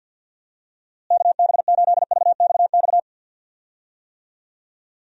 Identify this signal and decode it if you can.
Morse code, heard as a keyed tone, about 51 wpm, 700 Hz. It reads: KB8VXX
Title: USA Call Signs - 6 Characters - Repeated 2x - 50wpm